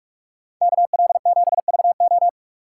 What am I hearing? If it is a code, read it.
Morse code, heard as a keyed tone, about 45 wpm, 700 Hz. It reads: KL7VO